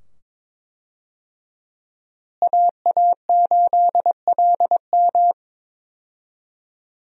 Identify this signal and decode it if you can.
Morse code, heard as a keyed tone, about 21 wpm, 700 Hz. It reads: AA8LM
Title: USA Call Signs - 5 Characters - Repeated 2x - 22wpm